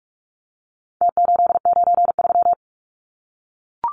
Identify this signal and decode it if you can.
Morse code, heard as a keyed tone, about 44 wpm, 700 Hz. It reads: T893
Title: Numbers - Contest Style - 4-Digit 45wpm